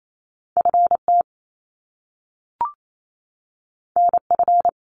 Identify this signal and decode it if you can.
Morse code, heard as a keyed tone, about 28 wpm, 700 Hz. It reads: FT NF
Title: Top Two Letter Combinations in Words 28wpm